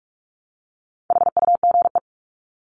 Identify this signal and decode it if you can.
Morse code, heard as a keyed tone, about 44 wpm, 700 Hz. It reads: HUGE
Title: Top 1001-1250 Words 45wpm